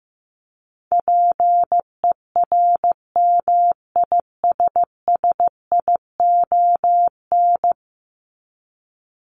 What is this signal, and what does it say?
Morse code, heard as a keyed tone, about 15 wpm, 700 Hz. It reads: PERMISSION
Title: Top 1501-1750 Words 15wpm